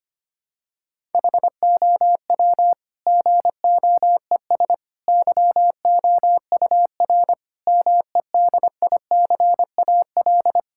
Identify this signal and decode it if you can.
Morse code, heard as a keyed tone, about 25 wpm, 700 Hz. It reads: HOW GOES YOUR MEDICAL
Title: Sentences from Top 1000 Words - Review Part 5 - 25wpm